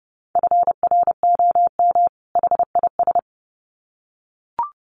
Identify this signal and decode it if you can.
Morse code, heard as a keyed tone, about 30 wpm, 700 Hz. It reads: FROM HIS
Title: Sets of 2 Words 30wpm